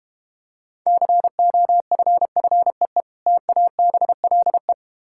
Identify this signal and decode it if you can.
Morse code, heard as a keyed tone, about 33 wpm, 700 Hz. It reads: COFFEE TABLE